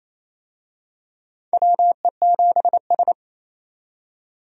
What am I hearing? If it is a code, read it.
Morse code, heard as a keyed tone, about 28 wpm, 700 Hz. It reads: WE7S
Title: USA Call Signs - 4 Characters - 28wpm